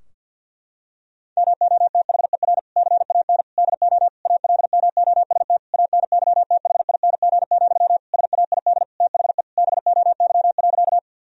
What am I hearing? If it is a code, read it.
Morse code, heard as a keyed tone, about 50 wpm, 700 Hz. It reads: MOTHER CAN DO ALMOST ANYTHING, SAID THE BOY.
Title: Sentences from Top 1000 Words - Review Part 4 - 50wpm